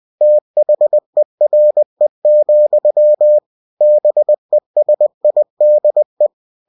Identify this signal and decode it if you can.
Morse code, heard as a keyed tone, about 20 wpm, 600 Hz. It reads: THERE, BESIDE
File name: cw-books / winnie-the-pooh / winnie-20wpm-page-0094